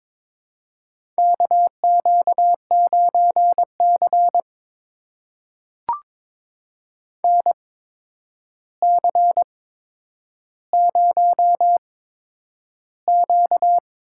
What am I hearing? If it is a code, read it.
Morse code, heard as a keyed tone, about 22 wpm, spaced as for 9 wpm, 700 Hz. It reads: KQ9C NC0Q
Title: USA Call Signs - 4 Characters - 8x Character Spacing - 22wpm